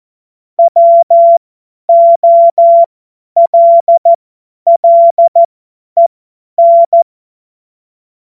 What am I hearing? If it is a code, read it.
Morse code, heard as a keyed tone, about 14 wpm, spaced as for 10 wpm, 680 Hz. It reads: WOLLEN
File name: Morsekurs / Witze / Witz 02. Mai 2023 50 F70